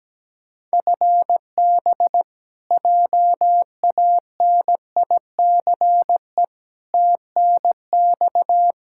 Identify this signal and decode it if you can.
Morse code, heard as a keyed tone, about 17 wpm, 700 Hz. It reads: FB JANICE TNX
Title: K1USN SST QSO Practice 17wpm